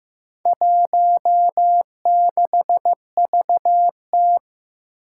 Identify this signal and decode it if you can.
Morse code, heard as a keyed tone, about 15 wpm, 700 Hz. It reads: 16VT
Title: Four Challenging Letter-Number Combinations 15wpm